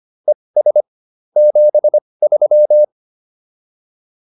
Morse code, heard as a keyed tone, about 25 wpm, spaced as for 20 wpm, 600 Hz. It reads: ES 73